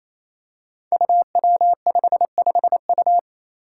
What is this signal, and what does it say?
Morse code, heard as a keyed tone, about 27 wpm, 700 Hz. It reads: UW55U
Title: Five Challenging Letter-Number Combinations 28wpm